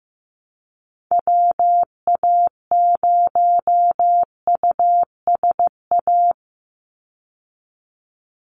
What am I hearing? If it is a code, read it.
Morse code, heard as a keyed tone, about 15 wpm, 700 Hz. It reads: WA0USA
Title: Call Signs Repeated 3x - 15wpm